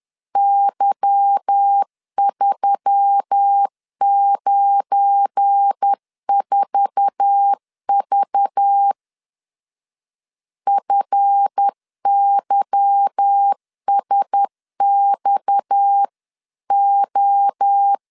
Morse code, heard as a keyed tone, about 10 wpm, 800 Hz. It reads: Y394V FYSXO
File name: LEZIONI / 8 YJ45 / 8 YJ4 10wpm 2633